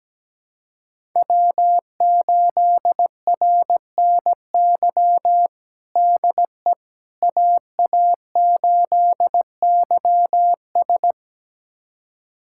Morse code, heard as a keyed tone, about 17 wpm, 700 Hz. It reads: W8RNY DE AA8YS
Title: USA Callsign DE Callsign 17wpm